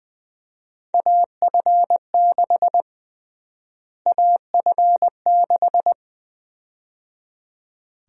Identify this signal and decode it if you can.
Morse code, heard as a keyed tone, about 20 wpm, 700 Hz. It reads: AF6 AF6
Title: USA Call Sign Prefixes - Repeated 2x - 20wpm